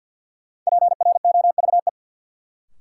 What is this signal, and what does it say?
Morse code, heard as a keyed tone, about 52 wpm, 700 Hz. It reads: PROVE
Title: Sentences from Top 1600 Words - New Word Drill Alt - 50wpm